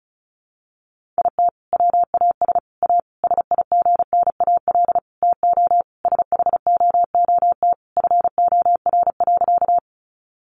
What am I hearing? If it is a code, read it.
Morse code, heard as a keyed tone, about 34 wpm, 700 Hz. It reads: IT WAS A SIGNAL TO SHOOT FOR.
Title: Sentences from Top 1400 Words 35wpm